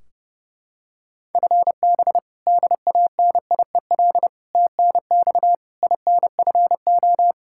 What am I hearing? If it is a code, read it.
Morse code, heard as a keyed tone, about 30 wpm, 700 Hz. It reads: FB DANIEL TNX INFO
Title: Practice QSO - Exchange 3 - More Info - 30wpm